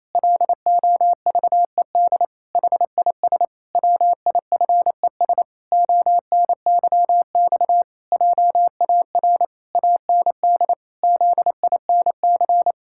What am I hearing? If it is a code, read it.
Morse code, heard as a keyed tone, about 28 wpm, 700 Hz. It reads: LOVED HIS WIFES ONYX JAR AND ZINC
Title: Pangram Challenge v2 - 28wpm